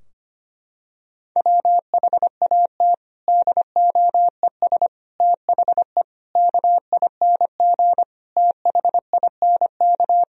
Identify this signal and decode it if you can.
Morse code, heard as a keyed tone, about 25 wpm, 700 Hz. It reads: WHAT DOES THE KING THINK
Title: Sentences from Top 1000 Words - Review Part 2 - 25wpm